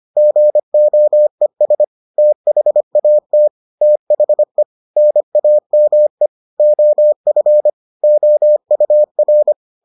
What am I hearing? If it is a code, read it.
Morse code, heard as a keyed tone, about 25 wpm, 600 Hz. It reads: GOES THAT THE NAME OF OUR